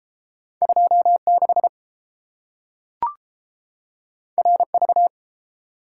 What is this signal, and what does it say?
Morse code, heard as a keyed tone, about 30 wpm, 700 Hz. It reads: 26 RV